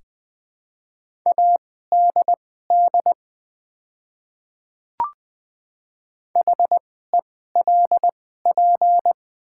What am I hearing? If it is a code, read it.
Morse code, heard as a keyed tone, about 20 wpm, spaced as for 15 wpm, 690 Hz. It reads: ADD HELP
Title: 2x Character Spacing - Top 200 Words - 20wpm